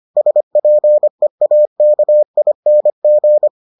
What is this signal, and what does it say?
Morse code, heard as a keyed tone, about 25 wpm, 600 Hz. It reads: SPEAKING